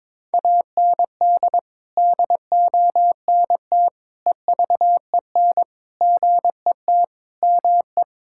Morse code, heard as a keyed tone, about 21 wpm, 700 Hz. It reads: AND DONT EVEN GET ME
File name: practice / 22wpm / Sets of 8 Words - 22wpm